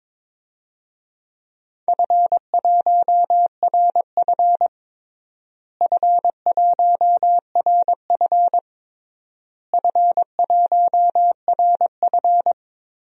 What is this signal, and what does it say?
Morse code, heard as a keyed tone, about 22 wpm, 700 Hz. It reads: F1RF F1RF F1RF
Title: DXCC-100 Call Signs - 4 Characters - Repeated 3x - 22wpm